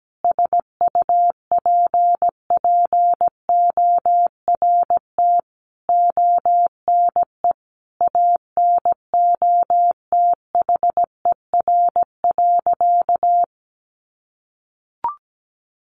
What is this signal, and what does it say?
Morse code, heard as a keyed tone, about 17 wpm, 700 Hz. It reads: SUPPORT ONE ANOTHER.
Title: Sentences from Top 1400 Words 17wpm